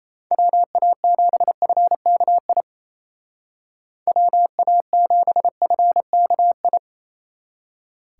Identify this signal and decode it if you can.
Morse code, heard as a keyed tone, about 34 wpm, 700 Hz. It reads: WA7FKI WA7FKI
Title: Speed Racing - USA 2x3 Call Signs 28wpm